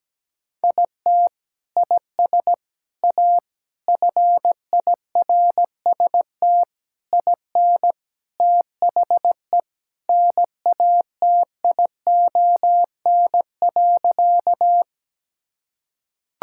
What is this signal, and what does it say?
Morse code, heard as a keyed tone, about 17 wpm, 700 Hz. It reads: IT IS A FIRST IN THE NATION.
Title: Sentences from Top 1000 Words - Review Part 6 - 17wpm